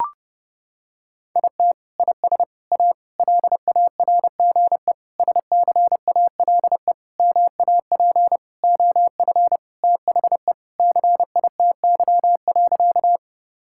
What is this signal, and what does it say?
Morse code, heard as a keyed tone, about 30 wpm, 700 Hz. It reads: IT IS A LARGE SCALE MAP OF THE CITY.
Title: Sentences from Top 1000 Words - Review Part 5 - 30wpm